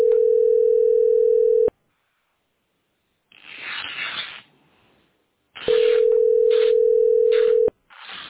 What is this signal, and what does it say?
Ringback tone (North America), 440 + 480 Hz.